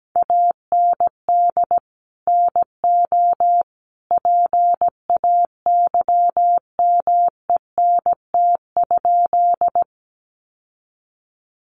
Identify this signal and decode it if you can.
Morse code, heard as a keyed tone, about 17 wpm, 700 Hz. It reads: AND NO PAYMENT?
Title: Sentences from Top 1200 Words 17wpm